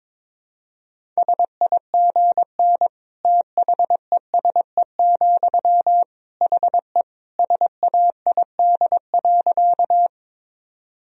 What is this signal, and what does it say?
Morse code, heard as a keyed tone, about 22 wpm, 700 Hz. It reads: SIGN THESE, HE SAID.
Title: Sentences from Top 1000 Words - Review Part 1 - 22wpm